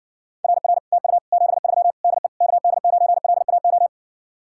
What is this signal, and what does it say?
Morse code, heard as a keyed tone, about 60 wpm, 700 Hz. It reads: RR TU 73 DE KD9LAO